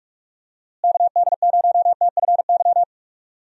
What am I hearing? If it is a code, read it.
Morse code, heard as a keyed tone, about 46 wpm, 700 Hz. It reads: KD0TFY
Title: USA Call Signs - 6 Characters - Repeated 2x - 45wpm